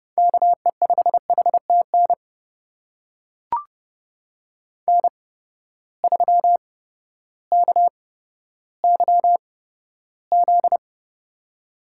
Morse code, heard as a keyed tone, about 30 wpm, spaced as for 14 wpm, 700 Hz. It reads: KE5HTN N3KYZ
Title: USA Call Signs - 8x Character Spacing - 30wpm